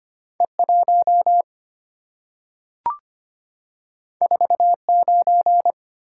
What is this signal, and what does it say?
Morse code, heard as a keyed tone, about 25 wpm, 690 Hz. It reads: E1 49
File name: Two Letter-Number Combinations - 25wpm